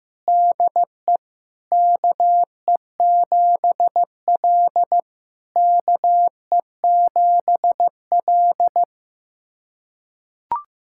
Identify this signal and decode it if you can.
Morse code, heard as a keyed tone, about 15 wpm, 700 Hz. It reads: DE KE7L KE7L
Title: SOTA QSO Practice 15wpm